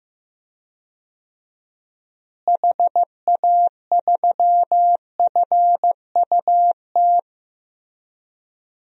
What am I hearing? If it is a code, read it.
Morse code, heard as a keyed tone, about 15 wpm, 700 Hz. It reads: HA3FUT
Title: DXCC-100 Call Signs - 6 Characters - Repeated 2x - 15wpm